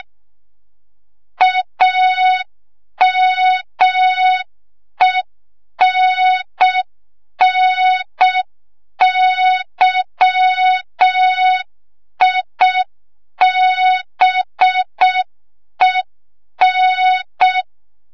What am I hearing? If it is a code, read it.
Morse code, heard as a keyed tone, about 6 wpm, 3700 Hz. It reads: AMENNYIBEN
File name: Morze-HA2MN / hang / 25 GYAKORLAT